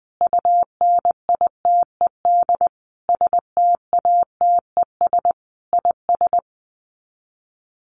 Morse code, heard as a keyed tone, about 20 wpm, 700 Hz. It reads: UNITED STATES IS
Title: Sets of 4 Words 20wpm